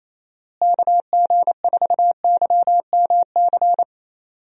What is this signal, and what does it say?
Morse code, heard as a keyed tone, about 28 wpm, 700 Hz. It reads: KG4YMC
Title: USA Call Signs - 6 Characters - Repeated 2x - 28wpm